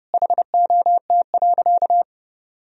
Morse code, heard as a keyed tone, about 30 wpm, 700 Hz. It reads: HOT.
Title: Sentences from Top 1000 Words - Review Part 3 - 30wpm